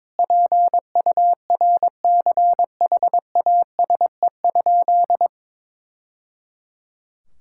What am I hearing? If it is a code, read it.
Morse code, heard as a keyed tone, about 22 wpm, 700 Hz. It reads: PURCHASE?
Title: Sentences from Top 1000 Words - Review Part 5 - 22wpm